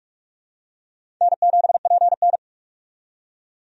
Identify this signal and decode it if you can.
Morse code, heard as a keyed tone, about 43 wpm, 700 Hz. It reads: N7PN
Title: USA Call Signs - 4 Characters - 45wpm